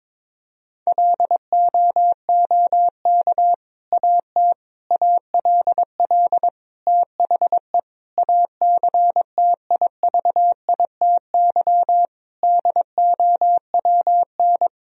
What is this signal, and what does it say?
Morse code, heard as a keyed tone, about 22 wpm, 700 Hz. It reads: LOOK AT ALL THE ACTIVITY DOWN
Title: Sentences from Top 1000 Words - Review Part 4 - 22wpm